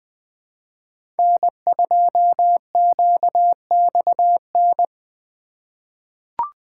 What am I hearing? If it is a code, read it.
Morse code, heard as a keyed tone, about 20 wpm, 700 Hz. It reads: N2QXN
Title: USA Call Signs - 5 Characters - 4x Character Spacing - 20wpm